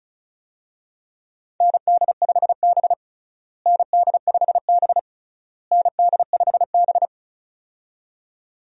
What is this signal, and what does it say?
Morse code, heard as a keyed tone, about 36 wpm, 700 Hz. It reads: ND5B ND5B ND5B
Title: USA Call Signs - 4 Characters - Repeated 3x - 35wpm